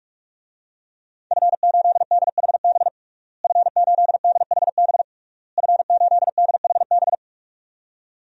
Morse code, heard as a keyed tone, about 41 wpm, 700 Hz. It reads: F8DHB F8DHB F8DHB
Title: DXCC-100 Call Signs - 5 Characters - Repeated 3x - 45wpm